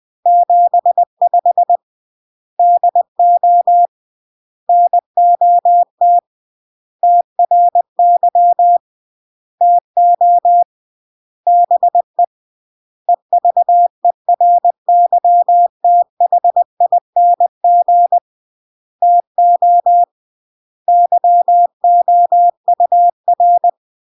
Morse code, heard as a keyed tone, about 20 wpm, 700 Hz. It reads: 75 DO NOT TRY TO BE EVERYTHING TO YOUR
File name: CW / jodorowsky / 82 commandments 20-20